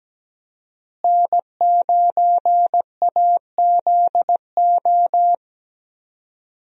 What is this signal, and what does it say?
Morse code, heard as a keyed tone, about 17 wpm, 700 Hz. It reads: N9AZO
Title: USA Call Signs - 5 Characters - Repeated 2x - 17wpm